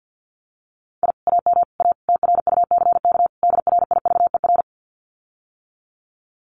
Morse code, heard as a keyed tone, about 48 wpm, 700 Hz. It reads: I AM A TRUCK DRIVER
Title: Common QSO Phrases Encore 50wpm